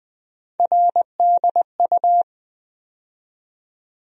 Morse code, heard as a keyed tone, about 20 wpm, 700 Hz. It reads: RDU
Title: USA Call Sign Postfixes - 3 Letters 20wpm